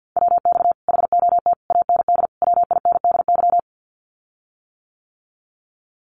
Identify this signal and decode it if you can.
Morse code, heard as a keyed tone, about 50 wpm, 700 Hz. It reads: WX HOT AND WINDY
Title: Common QSO Phrases Encore 50wpm